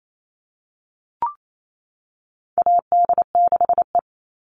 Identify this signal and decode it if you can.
Morse code, heard as a keyed tone, about 28 wpm, 700 Hz. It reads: AD6E
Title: Call Signs - 28wpm